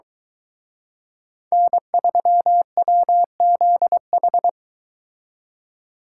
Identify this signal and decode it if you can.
Morse code, heard as a keyed tone, about 23 wpm, 700 Hz. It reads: N3WZH